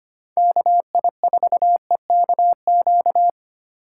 Morse code, heard as a keyed tone, about 25 wpm, 700 Hz. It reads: KI4EKQ
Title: USA Call Signs - 4x Character Spacing - 25wpm